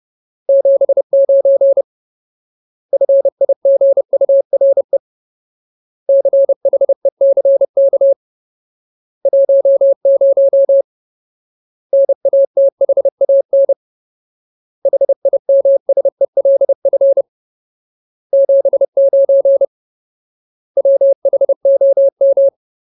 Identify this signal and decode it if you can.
Morse code, heard as a keyed tone, about 30 wpm, 550 Hz. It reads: 79 FIGURE CHECK 10 NATHAN HIMSELF 79 WHOM
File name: Ultimate Word List / 30WPM Ultimate